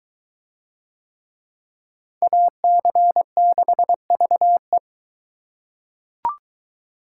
Morse code, heard as a keyed tone, about 22 wpm, 700 Hz. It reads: AC6VE